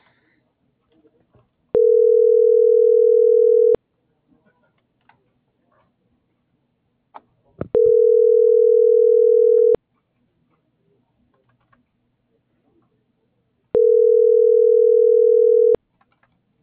Ringback tone (North America), 440 + 480 Hz.